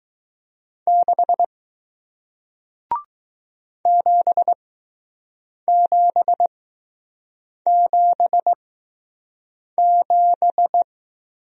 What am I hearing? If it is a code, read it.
Morse code, heard as a keyed tone, about 20 wpm, spaced as for 6 wpm, 700 Hz. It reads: 6 7777